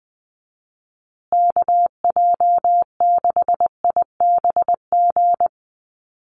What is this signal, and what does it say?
Morse code, heard as a keyed tone, about 20 wpm, 700 Hz. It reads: KJ6IBG
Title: USA Call Signs Repeated 2x 20wpm